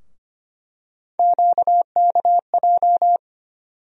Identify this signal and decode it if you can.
Morse code, heard as a keyed tone, about 25 wpm, 700 Hz. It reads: QKJ
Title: USA Call Sign Postfixes - 3 Letters 25wpm